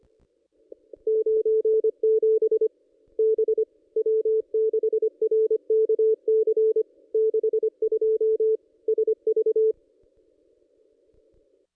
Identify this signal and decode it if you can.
Morse code, heard as a keyed tone, about 25 wpm, 440 Hz. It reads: 97 B W6RKC 62 SV
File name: contests / ss-cw-k5tr-op-n5rz -1600